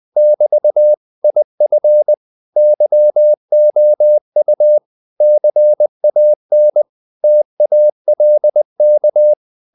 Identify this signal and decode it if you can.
Morse code, heard as a keyed tone, about 20 wpm, 600 Hz. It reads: <BT> IF YOU CAN TALK
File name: CW / Practice / If20